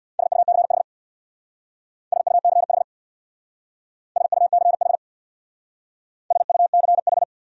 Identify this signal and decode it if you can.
Morse code, heard as a keyed tone, about 63 wpm, 700 Hz. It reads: SUCH SUCH SUCH SUCH